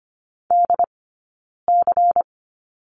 Morse code, heard as a keyed tone, about 25 wpm, 700 Hz. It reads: D C
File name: course / 25:10wpm / 092 Lesson 16d - Letter c - Two Characters 25wpm